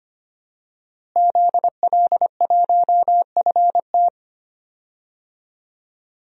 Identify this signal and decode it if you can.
Morse code, heard as a keyed tone, about 25 wpm, 700 Hz. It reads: ZL1FT
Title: DXCC-100 Call Signs - 5 Characters - Repeated 2x - 25wpm